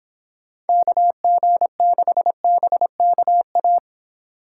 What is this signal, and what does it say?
Morse code, heard as a keyed tone, about 26 wpm, 700 Hz. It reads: KG6BKA